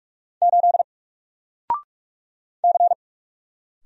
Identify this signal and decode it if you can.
Morse code, heard as a keyed tone, about 46 wpm, 700 Hz. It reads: 8 C
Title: Single Character (Letter:Number:Punctuation:Prosign) Alt 45wpm